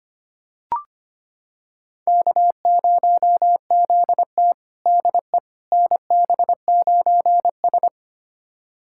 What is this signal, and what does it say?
Morse code, heard as a keyed tone, about 25 wpm, 700 Hz. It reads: K0ZT DE NB9S
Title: USA Callsign DE Callsign - 4 Characters - 25wpm